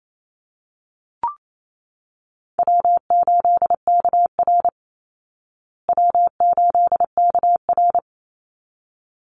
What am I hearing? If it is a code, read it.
Morse code, heard as a keyed tone, about 28 wpm, 690 Hz. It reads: W8KR W8KR
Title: Call Signs Repeated 3x - 28wpm